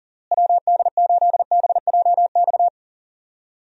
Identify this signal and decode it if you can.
Morse code, heard as a keyed tone, about 40 wpm, 700 Hz. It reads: WD8BJX
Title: USA Call Signs - 6 Characters - Repeated 3x - 40wpm